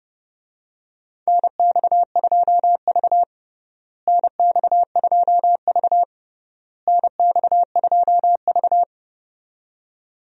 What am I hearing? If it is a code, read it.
Morse code, heard as a keyed tone, about 32 wpm, 700 Hz. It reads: NX2V NX2V NX2V
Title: USA Call Signs - 4 Characters - Repeated 3x - 30wpm